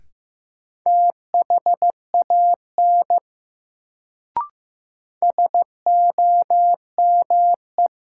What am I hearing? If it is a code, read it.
Morse code, heard as a keyed tone, about 15 wpm, 700 Hz. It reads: THAN SOME
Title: Sentences from Top 100 Words - New Word Drill - 15wpm